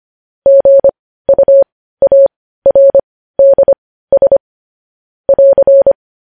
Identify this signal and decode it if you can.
Morse code, heard as a keyed tone, about 25 wpm, spaced as for 15 wpm, 550 Hz. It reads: GUARDS <AR>